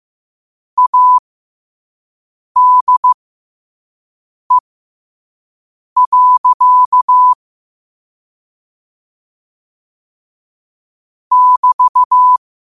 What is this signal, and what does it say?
Morse code, heard as a keyed tone, about 15 wpm, spaced as for 5 wpm, 1000 Hz. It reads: ADE. <BT>
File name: qotdfiles / 5wpm